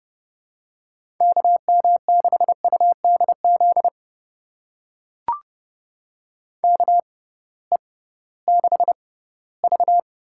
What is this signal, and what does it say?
Morse code, heard as a keyed tone, about 29 wpm, spaced as for 17 wpm, 700 Hz. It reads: KM6UDZ KE6V
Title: USA Call Signs - 6x Character Spacing - 30wpm